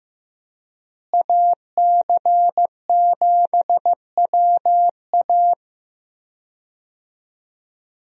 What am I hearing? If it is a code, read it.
Morse code, heard as a keyed tone, about 15 wpm, 700 Hz. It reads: AC7WA
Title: USA Call Signs - 5 Characters - Repeated 2x - 15wpm